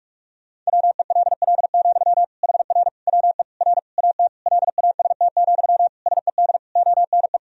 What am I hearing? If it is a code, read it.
Morse code, heard as a keyed tone, about 46 wpm, 700 Hz. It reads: WELL, HR WE R AT LAST, SED ONE
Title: Sentences from Top 300 Words and Abbreviations 45wpm